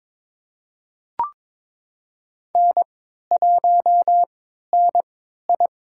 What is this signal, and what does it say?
Morse code, heard as a keyed tone, about 22 wpm, spaced as for 13 wpm, 700 Hz. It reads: N1NI
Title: USA Call Signs - 4 Characters - 3x Character Spacing - 22wpm